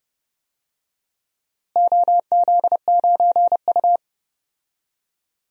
Morse code, heard as a keyed tone, about 32 wpm, 700 Hz. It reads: OZ9U
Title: DXCC-100 Call Signs - 4 Characters - Repeated 3x - 30wpm